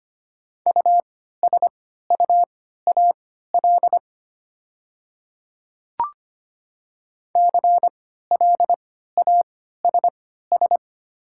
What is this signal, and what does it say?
Morse code, heard as a keyed tone, about 25 wpm, spaced as for 14 wpm, 700 Hz. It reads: USUAL CLASS
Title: 3x Character Spacing - Top 500 Words - 25wpm